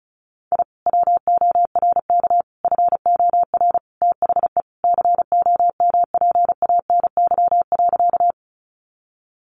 Morse code, heard as a keyed tone, about 34 wpm, 700 Hz. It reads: I WORK FOR THE COMPANY.
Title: Sentences from Top 600 Words 35wpm